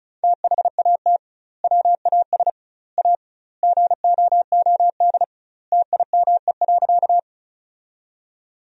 Morse code, heard as a keyed tone, about 34 wpm, 700 Hz. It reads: THAT WAS A GOOD TIME.
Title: 2x Word-Spacing - Sentences from Top 200 Words - 35wpm